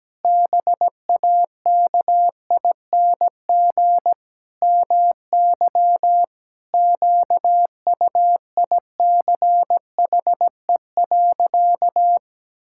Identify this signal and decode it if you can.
Morse code, heard as a keyed tone, about 17 wpm, 700 Hz. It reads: BAKING MY QUICHE.
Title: Pangram Challenge v2 - 17wpm